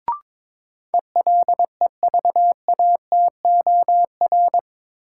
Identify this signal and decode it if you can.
Morse code, heard as a keyed tone, about 22 wpm, 700 Hz. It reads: ELEVATOR
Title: Sentences from Top 1600 Words - New Word Drill Alt - 22wpm